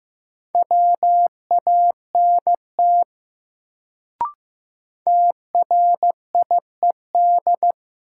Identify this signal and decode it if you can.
Morse code, heard as a keyed tone, about 15 wpm, 700 Hz. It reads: WANT TRIED
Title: Sets of 2 Words - New Word Drill - 15wpm